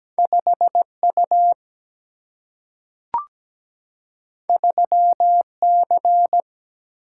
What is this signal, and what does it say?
Morse code, heard as a keyed tone, about 17 wpm, 700 Hz. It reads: 5U 3C
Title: Single-Letter Single-Number 17wpm